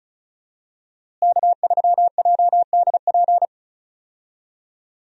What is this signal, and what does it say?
Morse code, heard as a keyed tone, about 34 wpm, 700 Hz. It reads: K3JDP
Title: USA Call Signs - 5 Characters - Repeated 2x - 35wpm